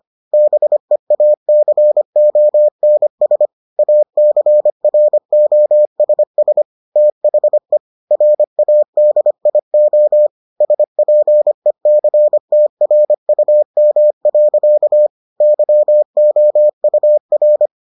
Morse code, heard as a keyed tone, about 25 wpm, 600 Hz. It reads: BEACONS ACROSS THE RADIO SPECTRUM. YOUR